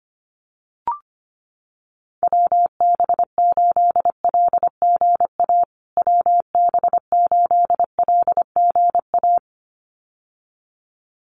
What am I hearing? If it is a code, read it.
Morse code, heard as a keyed tone, about 25 wpm, 700 Hz. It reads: WB8LGA WB8LGA
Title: USA Call Signs Repeated 2x 25wpm